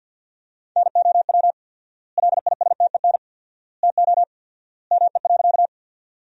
Morse code, heard as a keyed tone, about 50 wpm, 700 Hz. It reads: NOW LISTEN TO ME.
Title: 4x Word-Spacing - Sentences from Top 400 Words - 50wpm